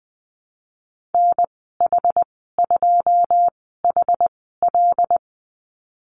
Morse code, heard as a keyed tone, about 20 wpm, spaced as for 14 wpm, 700 Hz. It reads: NH2HL
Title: 140 Lesson 23g - Letter f - Call Signs 20wpm